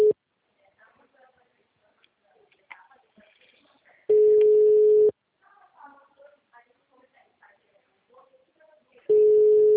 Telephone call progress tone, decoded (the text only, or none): ringback tone